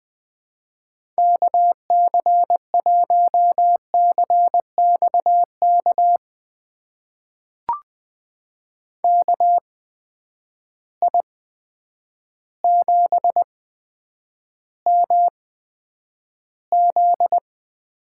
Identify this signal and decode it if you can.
Morse code, heard as a keyed tone, about 20 wpm, spaced as for 9 wpm, 700 Hz. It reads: KC1CXK KI7MZ
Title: USA Call Signs - 6 Characters - 8x Character Spacing - 20wpm